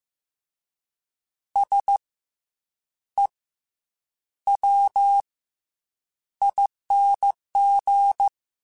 Morse code, heard as a keyed tone, about 15 wpm, 800 Hz. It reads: S E W ING
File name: Special INGs / Special INGs-mp3s / SlowToFaster ing @13